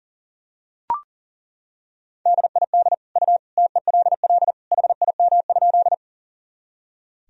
Morse code, heard as a keyed tone, about 37 wpm, 700 Hz. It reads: DID U TELL HIM?
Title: Sentences from Top 200 Words and Abbreviations 40wpm